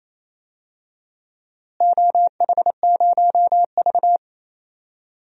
Morse code, heard as a keyed tone, about 28 wpm, 700 Hz. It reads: OH0V
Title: DXCC-100 Call Signs - 4 Characters - Repeated 2x - 28wpm